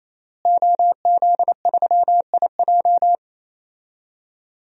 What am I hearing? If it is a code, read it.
Morse code, heard as a keyed tone, about 28 wpm, 700 Hz. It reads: OZ3IJ
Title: DXCC-100 Call Signs - 5 Characters - Repeated 3x - 28wpm